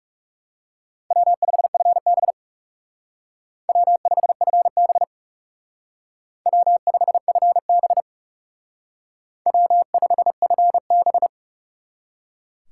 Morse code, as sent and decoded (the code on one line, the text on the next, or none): .-- ..... ..-. -... / .-- ..... ..-. -... / .-- ..... ..-. -... / .-- ..... ..-. -...
W5FB W5FB W5FB W5FB